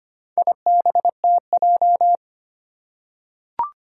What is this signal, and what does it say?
Morse code, heard as a keyed tone, about 25 wpm, 700 Hz. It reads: IBTJ
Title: Four Challenging Letter-Number Combinations 25wpm